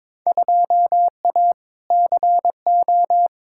Morse code, heard as a keyed tone, about 23 wpm, 700 Hz. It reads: 2A CO
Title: Field Day Exchanges - 22wpm